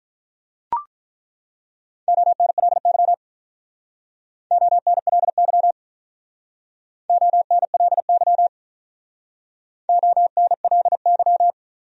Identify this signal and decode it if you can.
Morse code, heard as a keyed tone, about 48 wpm, 700 Hz. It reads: ONLY ONLY ONLY ONLY